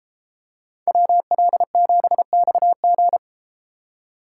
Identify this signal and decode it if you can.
Morse code, heard as a keyed tone, about 31 wpm, 700 Hz. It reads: WL7XG